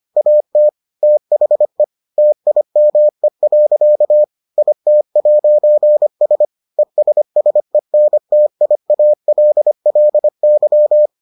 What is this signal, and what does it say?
Morse code, heard as a keyed tone, about 25 wpm, 600 Hz. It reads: AT THE TIME. IT'S ESSENTIALLY